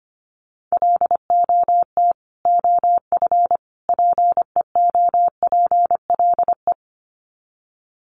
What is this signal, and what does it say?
Morse code, heard as a keyed tone, about 25 wpm, 700 Hz. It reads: LOT OF PEOPLE
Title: Sets of 3 Words 25wpm